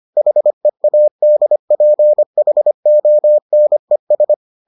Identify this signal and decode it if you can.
Morse code, heard as a keyed tone, about 25 wpm, 600 Hz. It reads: HEADPHONES